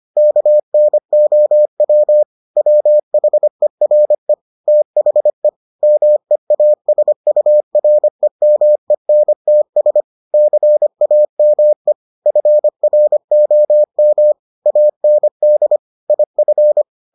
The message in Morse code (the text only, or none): KNOW WHERE THE MEASUREMENTS CAME FROM AND IF